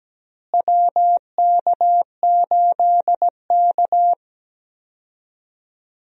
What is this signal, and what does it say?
Morse code, heard as a keyed tone, about 17 wpm, 700 Hz. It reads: WK8K
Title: USA Call Signs - 4 Characters - Repeated 2x - 17wpm